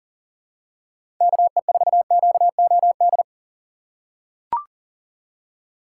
Morse code, heard as a keyed tone, about 40 wpm, 700 Hz. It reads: KE4QOD
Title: USA Call Signs - 6 Characters - 4x Character Spacing - 40wpm